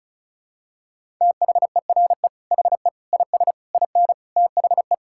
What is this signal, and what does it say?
Morse code, heard as a keyed tone, about 34 wpm, 700 Hz. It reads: THERE HE IS IN THE
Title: Sentences from Top 1600 Words 35wpm